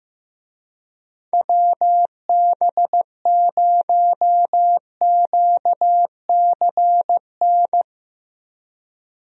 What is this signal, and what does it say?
Morse code, heard as a keyed tone, about 15 wpm, 700 Hz. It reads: WB0QCN
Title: USA Call Signs - 6 Characters - 15wpm